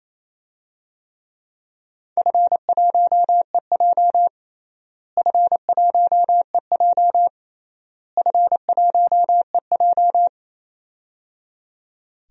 Morse code, heard as a keyed tone, about 28 wpm, 700 Hz. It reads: F1EJ F1EJ F1EJ
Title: DXCC-100 Call Signs - 4 Characters - Repeated 3x - 28wpm